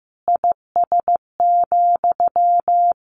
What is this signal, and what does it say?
Morse code, heard as a keyed tone, about 15 wpm, 700 Hz. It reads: IS,
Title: Sentences from Top 1500 Words 15wpm